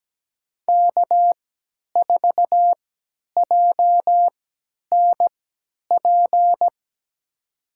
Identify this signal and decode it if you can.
Morse code, heard as a keyed tone, about 17 wpm, spaced as for 10 wpm, 700 Hz. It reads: K4JNP
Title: USA Call Signs - 3x Character Spacing - 17wpm